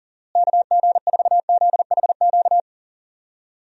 Morse code, heard as a keyed tone, about 40 wpm, 700 Hz. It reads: KG4ZHQ
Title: USA Call Signs - 6 Characters - Repeated 3x - 40wpm